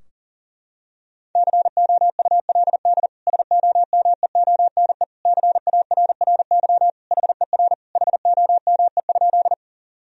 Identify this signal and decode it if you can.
Morse code, heard as a keyed tone, about 40 wpm, 700 Hz. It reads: COULD SOMEONE CARRY HER HOME?
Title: Sentences from Top 1000 Words - Review Part 4 - 40wpm